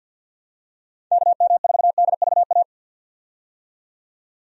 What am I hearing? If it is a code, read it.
Morse code, heard as a keyed tone, about 50 wpm, 700 Hz. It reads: KM4DVA
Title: USA Call Signs - 6 Characters - 50wpm